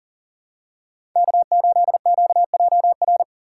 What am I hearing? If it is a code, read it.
Morse code, heard as a keyed tone, about 40 wpm, 700 Hz. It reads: K8QJR
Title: USA Call Signs - 5 Characters - 2x Character Spacing - 40wpm